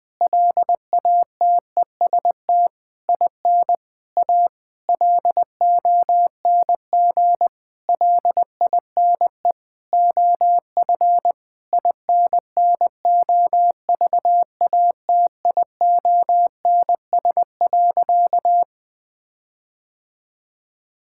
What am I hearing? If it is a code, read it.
Morse code, heard as a keyed tone, about 21 wpm, 700 Hz. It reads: LATEST IN A LONG LINE OF INNOVATIONS.
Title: Sentences with 8-Word Phrases - Spoken Beginning - 20wpm